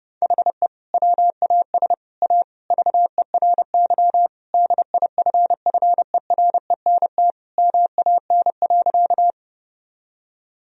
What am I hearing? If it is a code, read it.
Morse code, heard as a keyed tone, about 30 wpm, 700 Hz. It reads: HE WAS A VERY DIFFERENT MAN.
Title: Sentences from Top 1000 Words - Review Part 3 - 30wpm